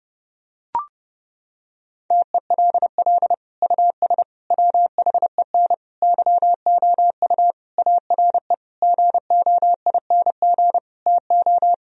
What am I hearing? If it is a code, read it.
Morse code, heard as a keyed tone, about 30 wpm, 700 Hz. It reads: TELL US WHEN YOU ARE GOING TO
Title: Sentences from Top 1000 Words - Review Part 2 - 30wpm